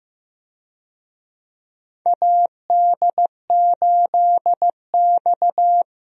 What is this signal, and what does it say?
Morse code, heard as a keyed tone, about 15 wpm, 700 Hz. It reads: AD8X
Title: USA Call Signs - 4 Characters - Repeated 3x - 15wpm